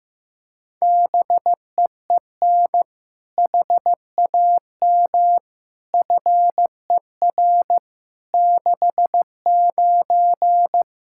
Morse code, heard as a keyed tone, about 15 wpm, 700 Hz. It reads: BEEN HAM FER 69
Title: Practice QSO - Exchange 3 - More Info - 15wpm